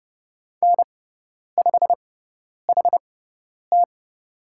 Morse code, heard as a keyed tone, about 30 wpm, spaced as for 10 wpm, 700 Hz. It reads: N5HT
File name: Lesson 16g - Letter c - Call Signs 30wpm